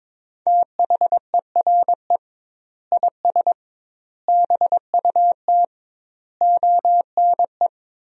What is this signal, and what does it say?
Morse code, heard as a keyed tone, about 22 wpm, 700 Hz. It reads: THERE IS BUT ONE
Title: 2x Word-Spacing - Sentences from Top 100 Words - 22wpm